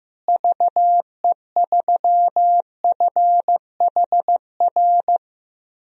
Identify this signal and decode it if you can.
Morse code, heard as a keyed tone, about 15 wpm, 700 Hz. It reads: VE3FHR
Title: DXCC-100 Call Signs - 6 Characters - Repeated 2x - 15wpm